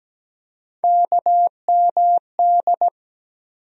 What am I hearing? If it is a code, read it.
Morse code, heard as a keyed tone, about 17 wpm, 700 Hz. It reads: KMD
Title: USA Call Sign Postfixes - 3 Letters 17wpm